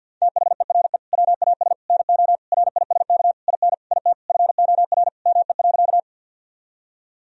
Morse code, heard as a keyed tone, about 50 wpm, 700 Hz. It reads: THERE WAS NO RISK IN IT FOR ME.
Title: Sentences from Top 1000 Words - Review Part 4 - 50wpm